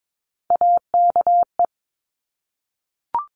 Morse code, heard as a keyed tone, about 22 wpm, 700 Hz. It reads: AKE
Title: Top Three Letter Combinations in Words 22wpm